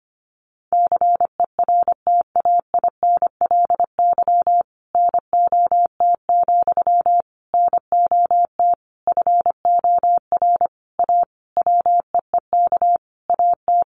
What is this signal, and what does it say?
Morse code, heard as a keyed tone, about 25 wpm, 700 Hz. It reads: CERTAINLY NOT, NOT FOR A WEEK AT
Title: Sentences from Top 900 Words 25wpm